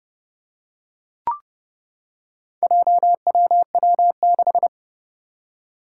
Morse code, heard as a keyed tone, about 31 wpm, 700 Hz. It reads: JWW6
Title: Four Challenging Letter-Number Combinations 30wpm